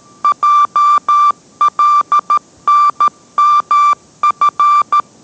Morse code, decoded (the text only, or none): JLNMF